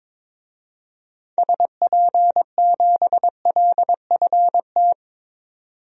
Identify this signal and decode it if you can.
Morse code, heard as a keyed tone, about 22 wpm, 700 Hz. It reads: SP7LFT
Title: DXCC-100 Call Signs - 6 Characters - 22wpm